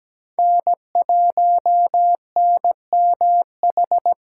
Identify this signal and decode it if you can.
Morse code, heard as a keyed tone, about 17 wpm, 700 Hz. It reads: N 1 N M H
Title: USA Call Signs - 5 Characters - 8x Character Spacing - 17wpm